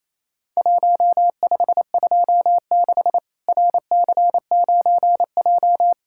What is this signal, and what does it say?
Morse code, heard as a keyed tone, about 28 wpm, 700 Hz. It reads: 1526 RC9J
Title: Worked All Europe DX Contest - QTC Exchange - 28wpm